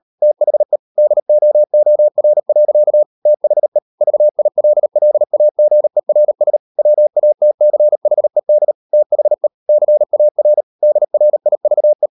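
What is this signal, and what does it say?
Morse code, heard as a keyed tone, about 38 wpm, 600 Hz. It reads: THE DOOR. THE VILLAGERS WATCHED THE CAR DRIVE
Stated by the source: Morse code practice at 38 WPM, sending Ken Follett's novel Never in English, from CW med Gnister.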